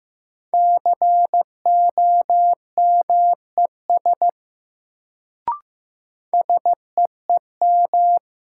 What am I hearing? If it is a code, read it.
Morse code, heard as a keyed tone, about 15 wpm, 700 Hz. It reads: COMES SEEM
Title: Sets of 3 Words - New Word Drill - 15wpm